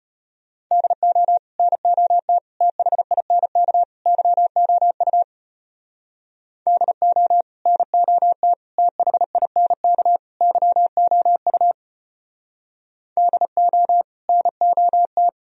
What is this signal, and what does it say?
Morse code, heard as a keyed tone, about 31 wpm, 700 Hz. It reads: DO NOT THINK YOU DO NOT THINK YOU DO NOT
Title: Speed-Racing - Sets of 4 Words 25wpm